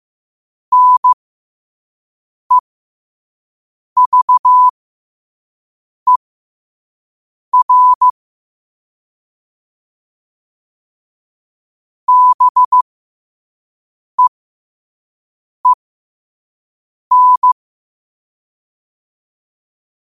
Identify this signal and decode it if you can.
Morse code, heard as a keyed tone, about 15 wpm, spaced as for 5 wpm, 1000 Hz. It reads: NEVER BEEN
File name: qotdfiles / 5wpm